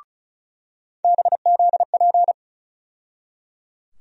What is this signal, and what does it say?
Morse code, heard as a keyed tone, about 36 wpm, 700 Hz. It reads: BZP
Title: USA Call Sign Postfixes - 3 Letters 35wpm